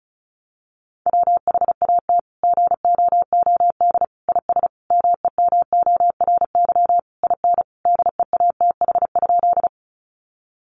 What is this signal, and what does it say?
Morse code, heard as a keyed tone, about 34 wpm, 700 Hz. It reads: WHAT GOOD IS MEMORY IN DEATH?
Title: Sentences from Top 1300 Words 35wpm